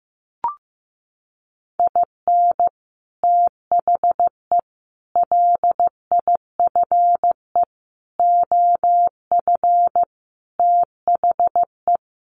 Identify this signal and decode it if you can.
Morse code, heard as a keyed tone, about 14 wpm, 700 Hz. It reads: IN THE LIFE OF THE
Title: Sets of 5 Words Encore2 15wpm